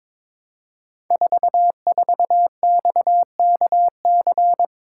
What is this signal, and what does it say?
Morse code, heard as a keyed tone, about 21 wpm, 700 Hz. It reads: 44XKC
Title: Code Groups - 5 Letter:Numbers - 22wpm